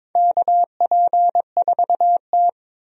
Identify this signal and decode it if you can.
Morse code, heard as a keyed tone, about 22 wpm, 700 Hz. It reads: KP4T
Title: DXCC-100 Call Signs - 4 Characters - 22wpm